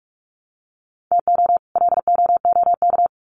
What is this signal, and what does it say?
Morse code, heard as a keyed tone, about 44 wpm, 700 Hz. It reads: TO LOOK
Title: Sets of 2 Words 45wpm